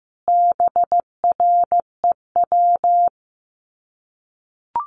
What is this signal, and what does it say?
Morse code, heard as a keyed tone, about 15 wpm, 700 Hz. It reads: BREW
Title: Common QSO Phrases 15wpm